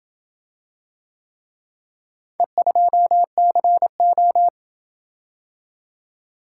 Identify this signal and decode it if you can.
Morse code, heard as a keyed tone, about 26 wpm, 700 Hz. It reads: E2CO